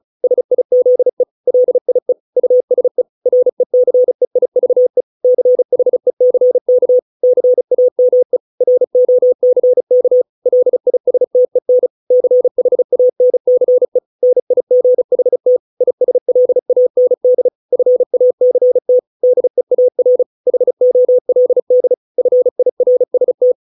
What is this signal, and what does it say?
Morse code, heard as a keyed tone, about 34 wpm, 500 Hz. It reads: SIZE LIE USE RECEIVE CHECK CAME ROCK LISTEN CHANCE NIGHT ISLAND FACT DEAR HOLD FIRST